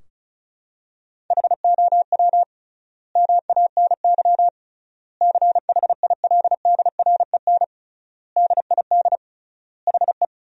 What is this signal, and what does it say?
Morse code, heard as a keyed tone, about 34 wpm, 700 Hz. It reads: HOW MANY CHILDREN DID HE
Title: 3x Word-Spacing - Sentences from Top 400 Words - 35wpm